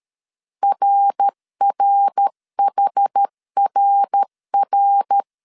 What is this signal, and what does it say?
Morse code, heard as a keyed tone, about 13 wpm, 800 Hz. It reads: RRHRR
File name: CW / DET / 1 3 1500C S A 12wpm 2905